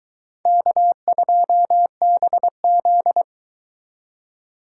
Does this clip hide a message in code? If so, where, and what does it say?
Yes: Morse code from 0.5 to 3.2 s, heard as a keyed tone, about 22 wpm, 700 Hz. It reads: K2BZ